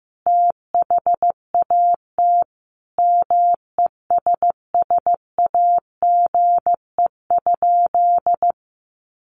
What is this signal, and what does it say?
Morse code, heard as a keyed tone, about 15 wpm, 700 Hz. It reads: THAT MESSAGE?
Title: Sentences from Top 1100 Words 15wpm